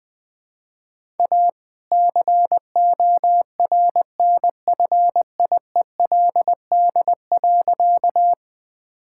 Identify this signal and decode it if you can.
Morse code, heard as a keyed tone, about 20 wpm, 700 Hz. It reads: A CORNFIELD.
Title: All-Age Appropriate Jokes - 20wpm